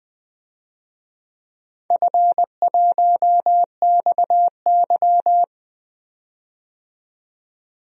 Morse code, heard as a keyed tone, about 20 wpm, 700 Hz. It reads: F1XY
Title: DXCC-100 Call Signs - 4 Characters - 20wpm